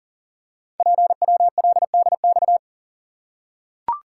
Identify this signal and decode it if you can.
Morse code, heard as a keyed tone, about 39 wpm, 700 Hz. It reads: PWLDX
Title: Code Groups - 5 Letter:Numbers - 40wpm